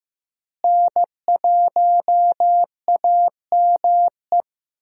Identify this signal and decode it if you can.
Morse code, heard as a keyed tone, about 15 wpm, 700 Hz. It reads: N1AME
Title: USA Call Signs - 5 Characters - Repeated 2x - 15wpm